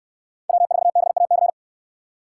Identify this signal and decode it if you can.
Morse code, heard as a keyed tone, about 69 wpm, 700 Hz. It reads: W4BAC